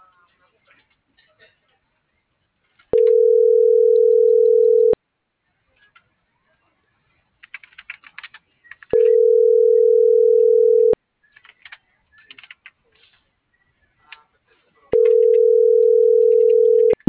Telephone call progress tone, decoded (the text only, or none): ringback tone